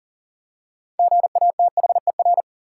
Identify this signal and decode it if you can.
Morse code, heard as a keyed tone, about 40 wpm, 700 Hz. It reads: GATHER
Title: Six Letter Words - 40wpm